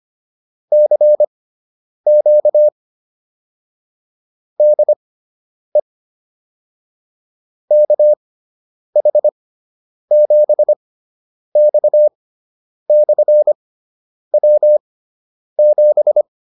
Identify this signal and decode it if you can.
Morse code, heard as a keyed tone, about 25 wpm, spaced as for 9 wpm, 600 Hz. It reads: CQ DE KH7X/W7